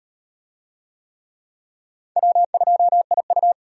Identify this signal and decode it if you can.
Morse code, heard as a keyed tone, about 38 wpm, 700 Hz. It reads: W2IU